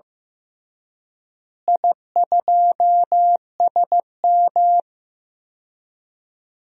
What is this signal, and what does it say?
Morse code, heard as a keyed tone, about 15 wpm, 700 Hz. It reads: I2SM
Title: DXCC-100 Call Signs - 4 Characters - Repeated 3x - 15wpm